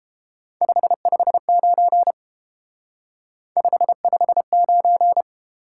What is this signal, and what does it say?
Morse code, heard as a keyed tone, about 30 wpm, 700 Hz. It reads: 559 559